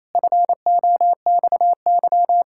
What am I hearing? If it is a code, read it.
Morse code, heard as a keyed tone, about 28 wpm, 700 Hz. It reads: FOXY
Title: Pangram Challenge v2 - 28wpm